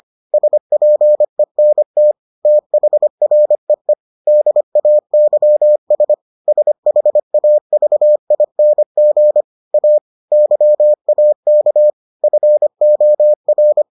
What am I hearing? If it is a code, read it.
Morse code, heard as a keyed tone, about 25 wpm, 600 Hz. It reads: SPENT THREE DAYS SHAVING A YAK FOR